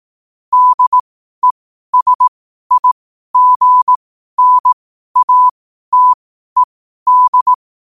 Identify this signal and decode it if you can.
Morse code, heard as a keyed tone, about 18 wpm, spaced as for 12 wpm, 1000 Hz. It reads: DESIGNATED
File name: qotdfiles / 12wpm